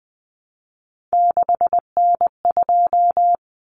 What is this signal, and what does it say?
Morse code, heard as a keyed tone, about 20 wpm, 700 Hz. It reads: 6N2
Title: Common Cut Numbers - 3-Digit 20wpm